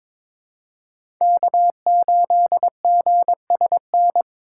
Morse code, heard as a keyed tone, about 22 wpm, 700 Hz. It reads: K8GSN
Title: USA Call Signs - 5 Characters - 4x Character Spacing - 22wpm